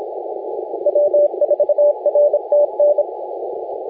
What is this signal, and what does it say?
Morse code, heard as a keyed tone, about 26 wpm, 590 Hz, with radio noise. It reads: W4RTN
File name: la4c / sac-cw-2018 / CONTEST AUDIO 1700Z